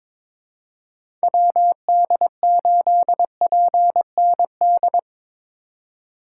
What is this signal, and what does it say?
Morse code, heard as a keyed tone, about 22 wpm, 700 Hz. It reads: WD8PND
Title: USA Call Signs - 6 Characters - 2x Character Spacing - 22wpm